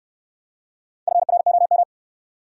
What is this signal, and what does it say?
Morse code, heard as a keyed tone, about 69 wpm, 690 Hz. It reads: FROM